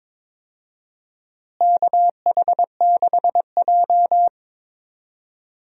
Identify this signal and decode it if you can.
Morse code, heard as a keyed tone, about 22 wpm, 700 Hz. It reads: KH6J
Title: USA Call Signs - 4 Characters - Repeated 3x - 22wpm